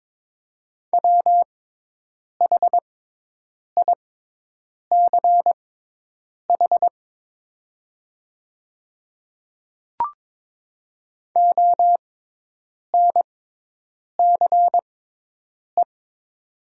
Morse code, heard as a keyed tone, about 23 wpm, spaced as for 8 wpm, 700 Hz. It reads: WHICH ONCE
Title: 6x Character Spacing - Top 300 Words - 22wpm